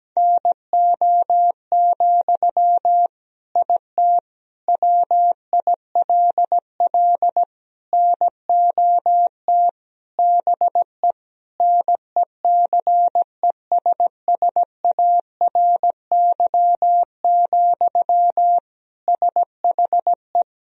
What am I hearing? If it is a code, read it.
Morse code, heard as a keyed tone, about 17 wpm, 700 Hz. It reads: NO, IT WILL NOT BE NECESSARY, SHE
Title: Sentences from Top 1000 Words - Review Part 4 - 17wpm